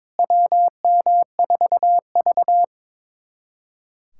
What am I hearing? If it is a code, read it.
Morse code, heard as a keyed tone, about 22 wpm, 700 Hz. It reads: WM4V
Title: USA Callsign DE Callsign - 4 Characters - 22wpm